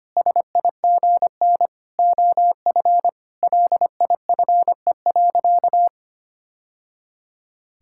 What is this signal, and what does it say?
Morse code, heard as a keyed tone, about 25 wpm, 700 Hz. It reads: SIGN OF LIFE.
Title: Sentences from Top 1000 Words - Review Part 6 - 25wpm